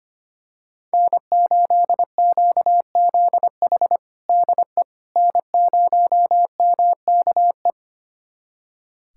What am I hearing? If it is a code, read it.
Morse code, heard as a keyed tone, about 25 wpm, 700 Hz. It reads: N8QZH DE N0MKE
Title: USA Callsign DE Callsign - 5 Characters - 25wpm